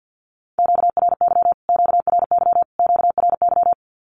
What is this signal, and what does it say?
Morse code, heard as a keyed tone, about 50 wpm, 700 Hz. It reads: QRY QRY QRY
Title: Q-Codes Repeated 3x 50wpm